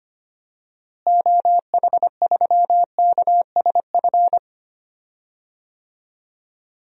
Morse code, heard as a keyed tone, about 25 wpm, 700 Hz. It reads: OH3KSF
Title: DXCC-100 Call Signs - 6 Characters - Repeated 2x - 25wpm